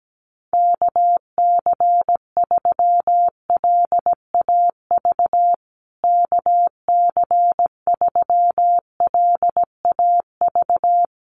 Morse code, heard as a keyed tone, about 17 wpm, 700 Hz. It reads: KC3LAV KC3LAV
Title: USA Call Signs Repeated 3x 17wpm